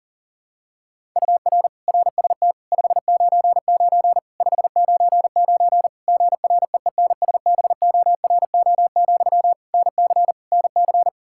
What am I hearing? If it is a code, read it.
Morse code, heard as a keyed tone, about 40 wpm, 700 Hz. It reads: UR RST 599 599 GREENSBORO, NC NC
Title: K1USN SST QSO Practice 40wpm